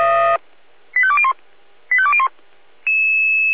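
Five-tone selective call (ZVEI), heard as quick runs of five tones, tones of about 70 ms. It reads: 85201 85201
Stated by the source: ZVEI five-tone selective call received with an SDR.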